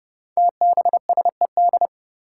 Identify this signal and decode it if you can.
Morse code, heard as a keyed tone, about 30 wpm, 700 Hz. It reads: TBSED
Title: Five Challenging Letter-Number Combinations 30wpm